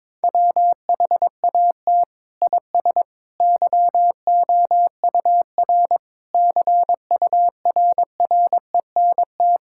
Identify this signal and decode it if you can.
Morse code, heard as a keyed tone, about 21 wpm, 700 Hz. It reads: WHAT IS YOUR CURRENT
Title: Sentences from Top 1000 Words - Review Part 6 - 22wpm